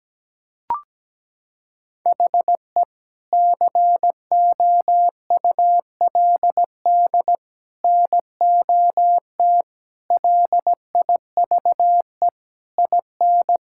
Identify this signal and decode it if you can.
Morse code, heard as a keyed tone, about 17 wpm, 700 Hz. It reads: HE COULD NOT LIVE IN
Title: Sentences from Top 1600 Words 17wpm